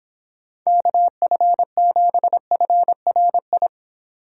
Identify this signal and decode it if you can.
Morse code, heard as a keyed tone, about 23 wpm, 700 Hz. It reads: KF7FRI